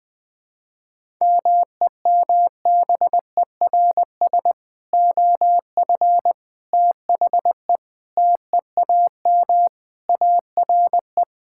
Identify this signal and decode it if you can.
Morse code, heard as a keyed tone, about 20 wpm, 700 Hz. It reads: MEMBERS OF THE TEAM ARE
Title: Sentences with 3-Word Phrases - 20wpm